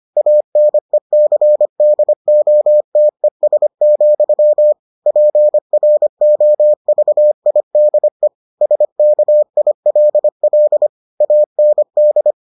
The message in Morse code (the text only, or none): ANECDOTES, PROVIDE SKILL AND